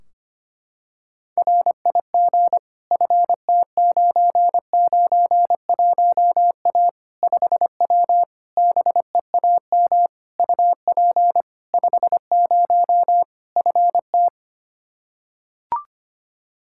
Morse code, heard as a keyed tone, about 25 wpm, 700 Hz. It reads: RIG FT991A 5W BEAM UP 50 FT
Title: Practice QSO - Exchange 2 - Info - 25wpm